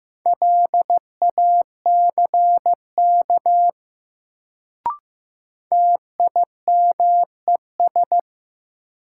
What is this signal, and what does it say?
Morse code, heard as a keyed tone, about 15 wpm, 700 Hz. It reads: LACK TIMES
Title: Sets of 3 Words - New Word Drill - 15wpm